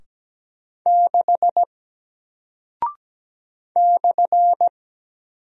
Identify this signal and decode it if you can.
Morse code, heard as a keyed tone, about 17 wpm, 700 Hz. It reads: 6 /
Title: Single Character (Letter:Number:Punctuation:Prosign) Alt 17wpm